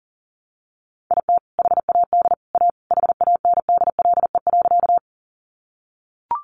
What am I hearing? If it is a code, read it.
Morse code, heard as a keyed tone, about 40 wpm, 700 Hz. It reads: IT HAD A HANDLE.
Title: Sentences from Top 1100 Words 40wpm